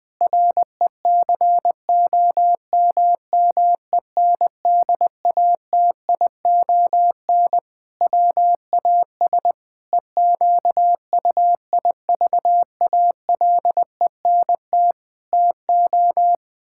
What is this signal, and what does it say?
Morse code, heard as a keyed tone, about 20 wpm, 700 Hz. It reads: RECOMMENDATION WAS EQUIVALENT TO
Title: Sentences from Top 1600 Words 20wpm